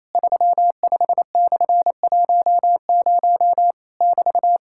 Morse code, heard as a keyed tone, about 28 wpm, 700 Hz. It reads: 35/10 <BT>
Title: Worked All Europe DX Contest - QTC Exchange - 28wpm